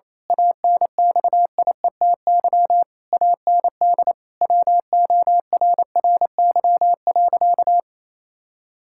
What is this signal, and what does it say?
Morse code, heard as a keyed tone, about 28 wpm, 700 Hz. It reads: ANXIETY AND WORRY.
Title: Pangram Challenge v2 - 28wpm